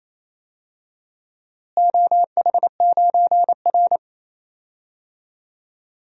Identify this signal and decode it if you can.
Morse code, heard as a keyed tone, about 28 wpm, 700 Hz. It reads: OH9R
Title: DXCC-100 Call Signs - 4 Characters - Repeated 3x - 28wpm